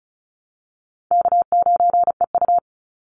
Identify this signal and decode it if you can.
Morse code, heard as a keyed tone, about 34 wpm, 700 Hz. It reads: K9EU
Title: Call Signs Repeated 3x - 35wpm